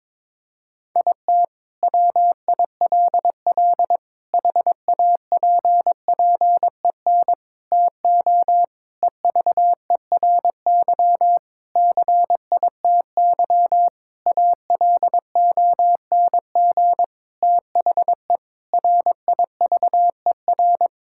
Morse code, heard as a keyed tone, about 22 wpm, 700 Hz. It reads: IT WILL HAPPEN TO EVERY CITY ALONG THE RIVER
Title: Sentences from Top 1000 Words - Review Part 6 - 22wpm